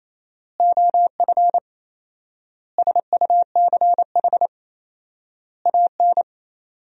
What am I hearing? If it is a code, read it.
Morse code, heard as a keyed tone, about 27 wpm, 700 Hz. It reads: OF SUCH AN
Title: 4x Word-Spacing - Sentences from Top 400 Words - 28wpm